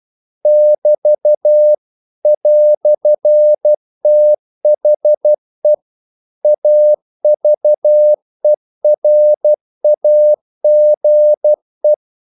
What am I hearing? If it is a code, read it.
Morse code, heard as a keyed tone, about 12 wpm, 600 Hz. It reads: <BT> "THE AVERAGE
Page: English quotations in CW (telegraphy - morse)